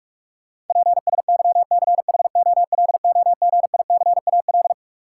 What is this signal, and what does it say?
Morse code, heard as a keyed tone, about 46 wpm, 700 Hz. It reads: PSYCHOLOGICAL
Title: Sentences from Top 1400 Words - New Word Drill Alt - 45wpm